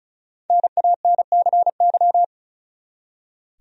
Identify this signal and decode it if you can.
Morse code, heard as a keyed tone, about 36 wpm, 700 Hz. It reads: NANCY
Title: USA Common Names - Max 5 Letters - 35wpm